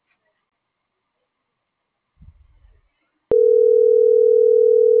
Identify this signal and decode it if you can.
Ringback tone (North America), 440 + 480 Hz.